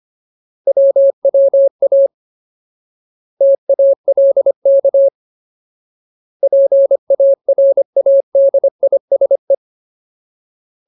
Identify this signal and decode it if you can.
Morse code, heard as a keyed tone, about 25 wpm, 550 Hz. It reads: WWA TALK PARADISE
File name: Ultimate Word List / 25WPM Ultimate